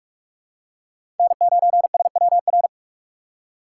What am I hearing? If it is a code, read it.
Morse code, heard as a keyed tone, about 46 wpm, 700 Hz. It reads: N9SWR
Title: USA Call Signs - 4x Character Spacing - 45wpm